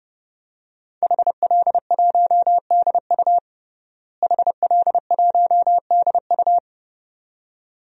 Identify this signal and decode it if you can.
Morse code, heard as a keyed tone, about 30 wpm, 700 Hz. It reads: HL1DU HL1DU
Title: DXCC-100 Call Signs - 5 Characters - Repeated 2x - 30wpm